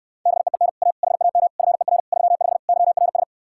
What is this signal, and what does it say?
Morse code, heard as a keyed tone, about 69 wpm, 700 Hz. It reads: BEEN A HAM FER 35 YRS